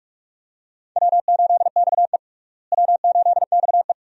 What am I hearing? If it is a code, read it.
Morse code, heard as a keyed tone, about 43 wpm, 700 Hz. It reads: W8XE W8XE
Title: USA Call Signs - 4 Characters - Repeated 2x - 45wpm